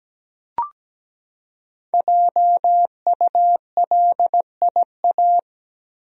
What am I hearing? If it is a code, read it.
Morse code, heard as a keyed tone, about 17 wpm, 700 Hz. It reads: JULIA
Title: USA Common Names - 17wpm